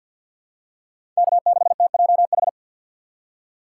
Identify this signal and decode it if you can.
Morse code, heard as a keyed tone, about 47 wpm, 700 Hz. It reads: K6TJH
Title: USA Call Signs - 5 Characters - 2x Character Spacing - 50wpm